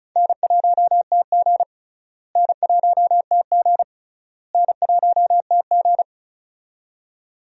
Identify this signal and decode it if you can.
Morse code, heard as a keyed tone, about 34 wpm, 700 Hz. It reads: N1TG N1TG N1TG
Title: USA Call Signs - 4 Characters - Repeated 3x - 35wpm